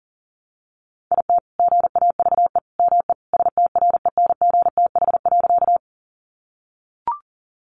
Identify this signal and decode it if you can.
Morse code, heard as a keyed tone, about 40 wpm, 700 Hz. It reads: IT GAVE ME STRENGTH.
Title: Sentences from Top 1300 Words 40wpm